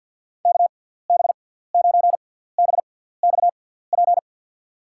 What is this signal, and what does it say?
Morse code, heard as a keyed tone, about 53 wpm, spaced as for 17 wpm, 700 Hz. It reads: KB9BXP
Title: USA Call Signs - 6 Characters - 6x Character Spacing - 50wpm